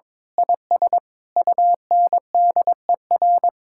Morse code, heard as a keyed tone, about 22 wpm, 700 Hz. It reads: IS UNDER
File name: Sentences with 9-Word Phrases - 22wpm